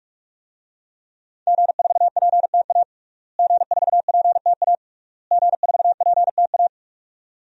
Morse code, heard as a keyed tone, about 46 wpm, 700 Hz. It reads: G4PTA G4PTA G4PTA
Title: DXCC-100 Call Signs - 5 Characters - Repeated 3x - 45wpm